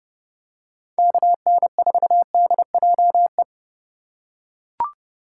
Morse code, heard as a keyed tone, about 30 wpm, 700 Hz. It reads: KN4DJE
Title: USA Call Signs - 6 Characters - 3x Character Spacing - 30wpm